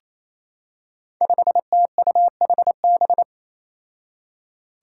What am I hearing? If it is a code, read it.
Morse code, heard as a keyed tone, about 28 wpm, 700 Hz. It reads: 5TUHB
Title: Five Challenging Letter-Number Combinations 28wpm